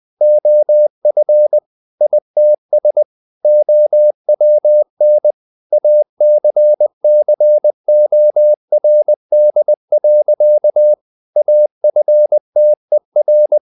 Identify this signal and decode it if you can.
Morse code, heard as a keyed tone, about 20 wpm, 600 Hz. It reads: OF ITS OWN ACCORD. AFTER
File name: the old man and the sea / the old man and the sea part 1 20wpm